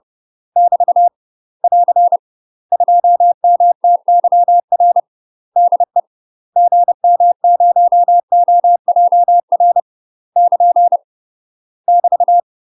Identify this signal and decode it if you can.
Morse code, heard as a keyed tone, about 30 wpm, 700 Hz. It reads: <BT> <AR> 2MTYR DE GM0OJR <KN> <BT>
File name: UK MASTER QSOs TRAINING / 30WPM / UK MASTER QSOs 0032